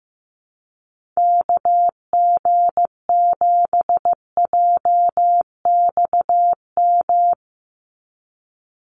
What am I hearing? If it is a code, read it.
Morse code, heard as a keyed tone, about 15 wpm, 700 Hz. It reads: KG7JXM
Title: USA Call Signs 15wpm